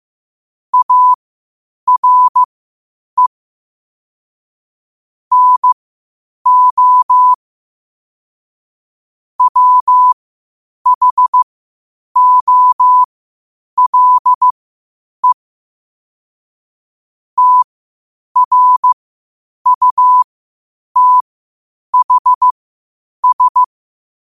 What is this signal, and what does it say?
Morse code, heard as a keyed tone, about 15 wpm, spaced as for 8 wpm, 1000 Hz. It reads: ARE NO WHOLE TRUTHS
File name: qotdfiles / 8wpm